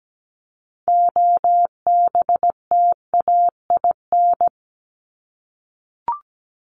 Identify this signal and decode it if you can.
Morse code, heard as a keyed tone, about 17 wpm, 700 Hz. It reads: OBTAIN
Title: ED Suffix - Top 100 Words with Base Words 17wpm